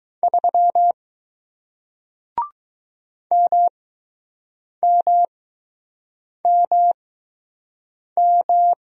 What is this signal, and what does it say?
Morse code, heard as a keyed tone, about 20 wpm, spaced as for 6 wpm, 700 Hz. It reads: 3 MMMM